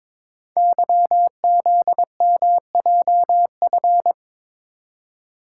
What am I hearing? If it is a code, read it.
Morse code, heard as a keyed tone, about 22 wpm, 700 Hz. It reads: YZMJF
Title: Code Groups - 5 Letters - 22wpm